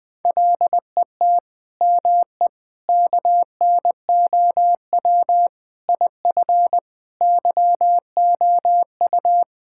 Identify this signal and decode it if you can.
Morse code, heard as a keyed tone, about 20 wpm, 700 Hz. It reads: LET ME KNOW IF YOU
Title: Sets of 8 Words - 20wpm